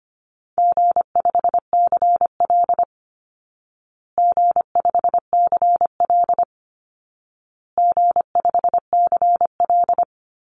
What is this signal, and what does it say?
Morse code, heard as a keyed tone, about 25 wpm, 700 Hz. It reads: G5CL G5CL G5CL
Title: Call Signs Repeated 2x - 25wpm